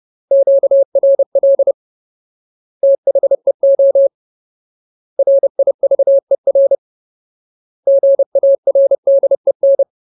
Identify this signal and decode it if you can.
Morse code, heard as a keyed tone, about 30 wpm, 550 Hz. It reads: QRL THEO RIVER GARDEN
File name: Ultimate Word List / 30WPM Ultimate